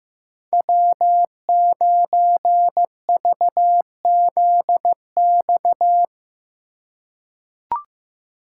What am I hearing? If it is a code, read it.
Morse code, heard as a keyed tone, about 15 wpm, 700 Hz. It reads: W9VZX
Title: USA Call Signs - 5 Characters - 2x Character Spacing - 15wpm